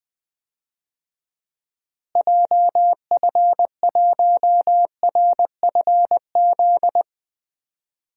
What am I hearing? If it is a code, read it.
Morse code, heard as a keyed tone, about 20 wpm, 700 Hz. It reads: JF1RFZ
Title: DXCC-100 Call Signs - 6 Characters - Repeated 2x - 20wpm